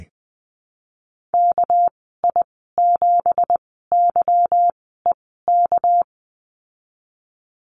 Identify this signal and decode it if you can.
Morse code, heard as a keyed tone, about 20 wpm, spaced as for 15 wpm, 700 Hz. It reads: KI7YEK
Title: 213 Lesson 34g - Number 8 - Call Signs 20wpm